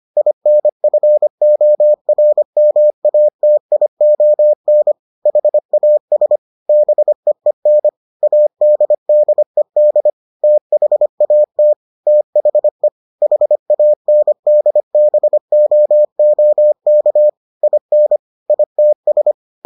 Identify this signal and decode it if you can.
Morse code, heard as a keyed tone, about 25 wpm, 600 Hz. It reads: INFORMATION HAS BEEN ADDED THAT THE HANDBOOK IN ITS